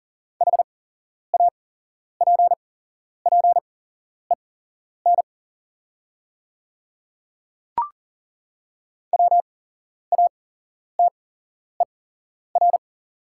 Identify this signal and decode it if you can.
Morse code, heard as a keyed tone, about 40 wpm, spaced as for 11 wpm, 690 Hz. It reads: HAPPEN WATER
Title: 8x Character Spacing - Top 500 Words - 40wpm